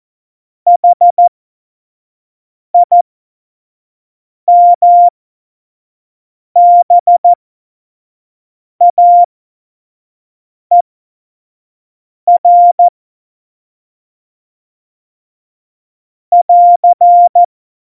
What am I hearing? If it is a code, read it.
Morse code, heard as a keyed tone, about 14 wpm, spaced as for 5 wpm, 700 Hz. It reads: HIMBAER <AR>
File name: Morsekurs / Witze / Witz 20. November 2018, 25 F70